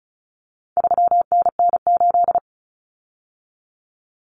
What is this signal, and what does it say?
Morse code, heard as a keyed tone, about 34 wpm, 700 Hz. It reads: 3NN8
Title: Common Cut Numbers - 4-Digit 35wpm